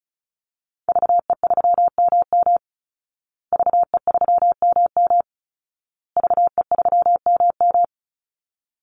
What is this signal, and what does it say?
Morse code, heard as a keyed tone, about 34 wpm, 700 Hz. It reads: VE3MM VE3MM VE3MM
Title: Call Signs Repeated 2x - 35wpm